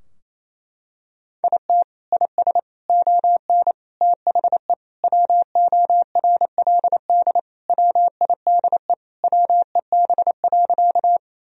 Morse code, heard as a keyed tone, about 28 wpm, 700 Hz. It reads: IT IS ON THE WORLD WIDE WEB.
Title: Sentences from Top 1000 Words - Review Part 4 - 28wpm